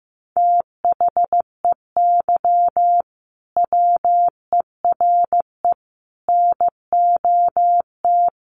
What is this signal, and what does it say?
Morse code, heard as a keyed tone, about 15 wpm, 700 Hz. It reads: THEY WERE NOT
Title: Sentences from Top 1000 Words 15wpm